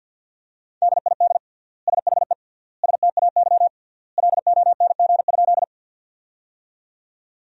Morse code, heard as a keyed tone, about 53 wpm, 700 Hz. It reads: DID SHE STAY LONG?
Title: 3x Word-Spacing - Sentences from Top 500 Words - 50wpm